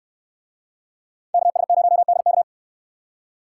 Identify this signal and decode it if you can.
Morse code, heard as a keyed tone, about 62 wpm, 700 Hz. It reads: KS0DK